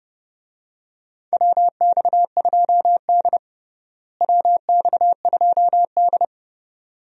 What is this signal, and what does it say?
Morse code, heard as a keyed tone, about 30 wpm, 700 Hz. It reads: WX2D WX2D
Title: USA Call Signs - 4 Characters - Repeated 2x - 30wpm